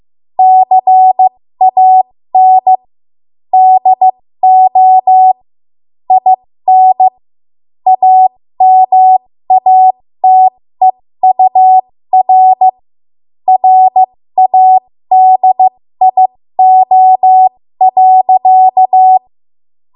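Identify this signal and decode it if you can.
Morse code, heard as a keyed tone, about 15 wpm, spaced as for 13 wpm, 750 Hz. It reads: CAN DO IN AMATEUR RADIO.
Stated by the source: 13 wpm Morse code practice archive transmission.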